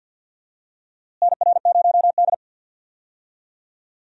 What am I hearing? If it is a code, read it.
Morse code, heard as a keyed tone, about 46 wpm, 700 Hz. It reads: NR0D
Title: USA Call Signs - 4 Characters - Repeated 3x - 50wpm